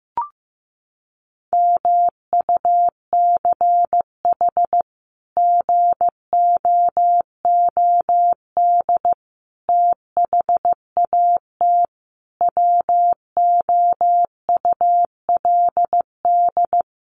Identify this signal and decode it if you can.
Morse code, heard as a keyed tone, about 15 wpm, 700 Hz. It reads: MUCH GOOD THAT WOULD
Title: Sentences from Top 200 Words 15wpm